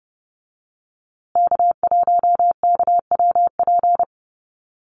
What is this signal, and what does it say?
Morse code, heard as a keyed tone, about 30 wpm, 700 Hz. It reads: K1KWP
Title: USA Call Signs Repeated 3x 30wpm